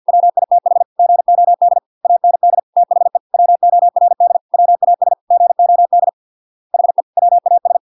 Morse code, heard as a keyed tone, about 50 wpm, 700 Hz. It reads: WITH GOD AND THE WORD WAS GOD HE WAS